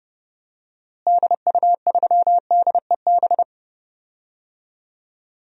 Morse code, heard as a keyed tone, about 32 wpm, 700 Hz. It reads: DU3DEB
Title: DXCC-100 Call Signs - 6 Characters - Repeated 3x - 30wpm